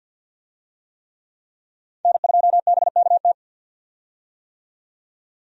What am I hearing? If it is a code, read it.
Morse code, heard as a keyed tone, about 51 wpm, 700 Hz. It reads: N2BKT
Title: USA Call Signs - 5 Characters - 50wpm